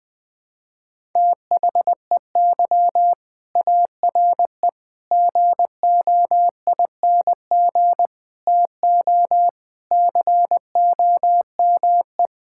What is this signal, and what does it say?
Morse code, heard as a keyed tone, about 21 wpm, 700 Hz. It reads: THEY ARE GOING TO COME
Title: Sentences with 3-Word Phrases - 20wpm